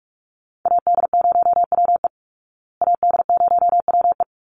Morse code, heard as a keyed tone, about 44 wpm, 700 Hz. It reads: AD0WE AD0WE
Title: Call Signs Repeated 2x - 45wpm